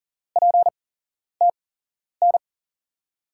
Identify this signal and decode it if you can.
Morse code, heard as a keyed tone, about 40 wpm, spaced as for 11 wpm, 700 Hz. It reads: PTN
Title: USA Call Signs - 6 Characters - 8x Character Spacing - 40wpm